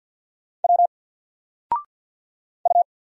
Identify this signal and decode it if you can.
Morse code, heard as a keyed tone, about 50 wpm, 700 Hz. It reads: W U
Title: Single Character (Letter:Number:Punctuation:Prosign) 50wpm